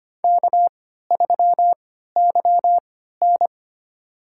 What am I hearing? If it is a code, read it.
Morse code, heard as a keyed tone, about 25 wpm, spaced as for 14 wpm, 700 Hz. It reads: K3YN
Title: USA Call Signs - 4 Characters - 3x Character Spacing - 25wpm